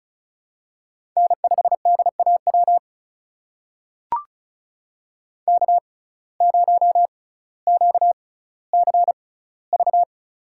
Morse code, heard as a keyed tone, about 36 wpm, spaced as for 18 wpm, 700 Hz. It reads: N5DAW K0QCV
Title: USA Call Signs - 5 Characters - 6x Character Spacing - 35wpm